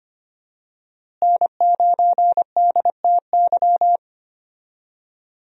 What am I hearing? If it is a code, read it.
Morse code, heard as a keyed tone, about 25 wpm, 700 Hz. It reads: N9DTY
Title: USA Call Signs - 5 Characters - Repeated 3x - 25wpm